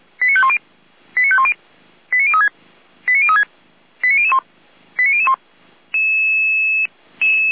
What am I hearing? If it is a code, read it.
Five-tone selective call (ZVEI), heard as quick runs of five tones, tones of about 70 ms. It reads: 89510 89510 89036 89036 89001 89001